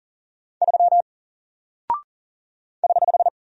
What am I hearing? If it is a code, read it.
Morse code, heard as a keyed tone, about 39 wpm, 700 Hz. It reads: 3 <HH>
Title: Single Character (Letter:Number:Punctuation:Prosign) Alt 40wpm